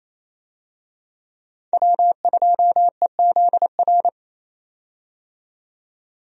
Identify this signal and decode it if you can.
Morse code, heard as a keyed tone, about 28 wpm, 700 Hz. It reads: W2EZR
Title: USA Call Signs - 5 Characters - 28wpm